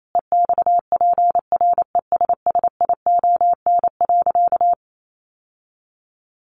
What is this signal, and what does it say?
Morse code, heard as a keyed tone, about 28 wpm, 700 Hz. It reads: EXPRESSION.
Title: Sentences from Top 1200 Words 28wpm